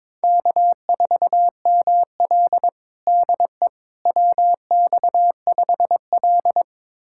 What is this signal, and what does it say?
Morse code, heard as a keyed tone, about 21 wpm, 700 Hz. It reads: K4ML DE WX5L
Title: USA Callsign DE Callsign - 4 Characters - 22wpm